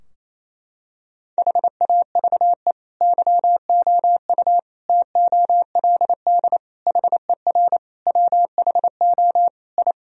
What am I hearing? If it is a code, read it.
Morse code, heard as a keyed tone, about 28 wpm, 700 Hz. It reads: HAVE YOU TOLD HER WHO I
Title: Sentences from Top 1000 Words - Review Part 2 - 28wpm